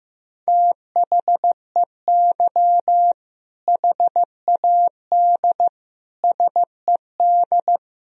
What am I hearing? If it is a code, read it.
Morse code, heard as a keyed tone, about 15 wpm, 700 Hz. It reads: THEY HAD SED
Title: Sentences from Top 400 Words and Abbreviations 15wpm